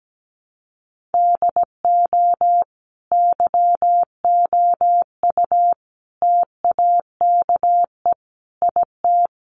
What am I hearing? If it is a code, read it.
Morse code, heard as a keyed tone, about 17 wpm, 700 Hz. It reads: DO YOU TAKE IT
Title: Sentences from Top 1200 Words 17wpm